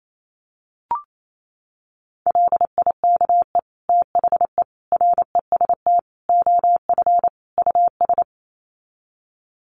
Morse code, heard as a keyed tone, about 28 wpm, 700 Hz. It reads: LIKE THE REST OF US
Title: Sets of 5 Words Encore2 28wpm